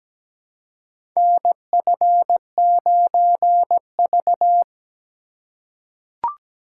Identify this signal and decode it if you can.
Morse code, heard as a keyed tone, about 17 wpm, 700 Hz. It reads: NF9V
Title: USA Call Signs - 4 Characters - 3x Character Spacing - 17wpm